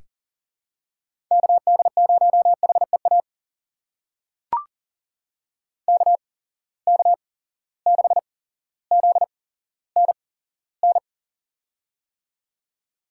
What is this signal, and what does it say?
Morse code, heard as a keyed tone, about 40 wpm, spaced as for 17 wpm, 700 Hz. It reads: KD0HEA KK6ZNN
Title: USA Call Signs - 6 Characters - 8x Character Spacing - 40wpm